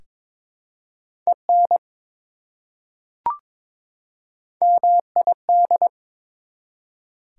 Morse code, heard as a keyed tone, about 22 wpm, 700 Hz. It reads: EN MID
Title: Prefixes and Suffixes 22wpm